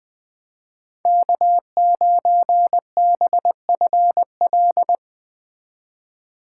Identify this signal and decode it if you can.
Morse code, heard as a keyed tone, about 20 wpm, 700 Hz. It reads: K9BFL